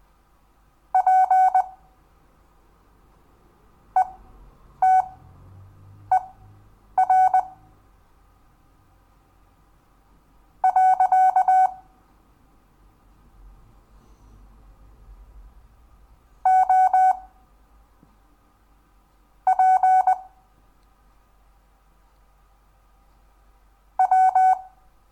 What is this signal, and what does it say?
Morse code, heard as a keyed tone, about 20 wpm, spaced as for 5 wpm, 750 Hz. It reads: PETER. OP W
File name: cw / Lesson10 REVIEW